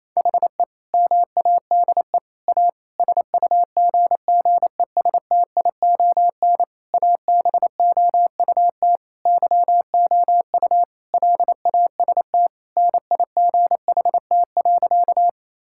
Morse code, heard as a keyed tone, about 28 wpm, 700 Hz. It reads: HE MADE A SUGGESTION ABOUT YOU LAST NIGHT.
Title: Sentences from Top 1600 Words 28wpm